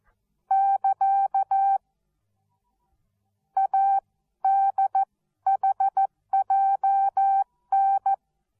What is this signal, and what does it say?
Morse code, heard as a keyed tone, about 14 wpm, spaced as for 12 wpm, 780 Hz. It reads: <KA> ADHJN
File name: Onlinelehrgang / morsen / DJ4UF Morsen13